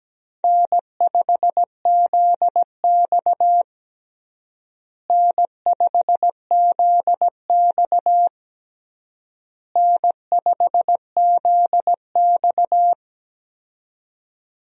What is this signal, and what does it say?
Morse code, heard as a keyed tone, about 17 wpm, 700 Hz. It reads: N5ZX N5ZX N5ZX
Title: USA Call Signs - 4 Characters - Repeated 3x - 17wpm